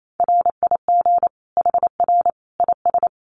Morse code, heard as a keyed tone, about 28 wpm, 700 Hz. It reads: RIG HR IS
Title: Common QSO Phrases 28wpm